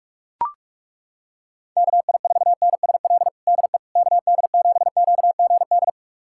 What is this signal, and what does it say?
Morse code, heard as a keyed tone, about 41 wpm, 700 Hz. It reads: KI4NSL DE KD7QGD
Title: USA Callsign DE Callsign - 6 Characters - 45wpm